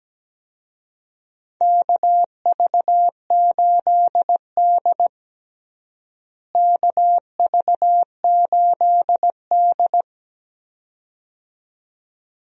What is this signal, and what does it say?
Morse code, heard as a keyed tone, about 17 wpm, 700 Hz. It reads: KV8D KV8D
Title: USA Call Signs - 4 Characters - Repeated 2x - 17wpm